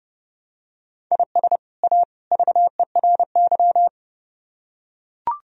Morse code, heard as a keyed tone, about 30 wpm, 700 Hz. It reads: IS A VERY